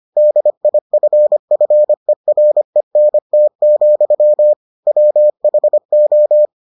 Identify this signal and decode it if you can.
Morse code, heard as a keyed tone, about 25 wpm, 600 Hz. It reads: DIFFERENT, WHO